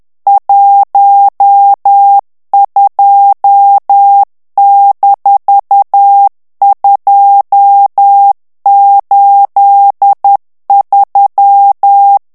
Morse code, heard as a keyed tone, about 10 wpm, 800 Hz. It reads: 12-283
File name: ANGLAIS texte clair 3 vitesse 10 mots minute